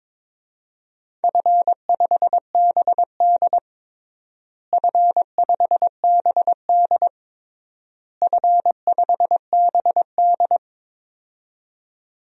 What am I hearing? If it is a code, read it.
Morse code, heard as a keyed tone, about 22 wpm, 700 Hz. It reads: F5BD F5BD F5BD
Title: DXCC-100 Call Signs - 4 Characters - Repeated 3x - 22wpm